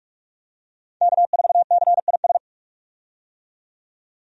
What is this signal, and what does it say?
Morse code, heard as a keyed tone, about 43 wpm, 700 Hz. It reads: K4CIS
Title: USA Call Signs - 5 Characters - 45wpm